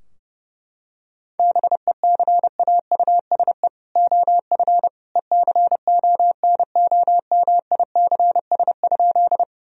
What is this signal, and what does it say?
Morse code, heard as a keyed tone, about 30 wpm, 700 Hz. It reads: BECAUSE OF ECONOMICS?
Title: Sentences from Top 1000 Words - Review Part 3 - 30wpm